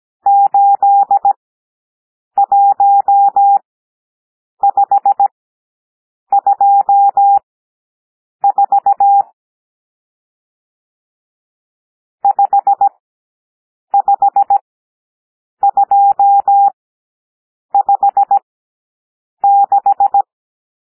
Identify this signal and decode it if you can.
Morse code, heard as a keyed tone, about 17 wpm, spaced as for 7 wpm, 800 Hz. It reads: 81524 55256
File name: SOUND NUMBERS 17 wpm